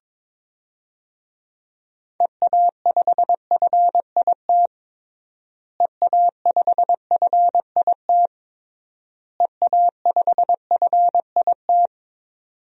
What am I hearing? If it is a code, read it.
Morse code, heard as a keyed tone, about 21 wpm, 700 Hz. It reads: EA5FIT EA5FIT EA5FIT
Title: DXCC-100 Call Signs - 6 Characters - Repeated 3x - 22wpm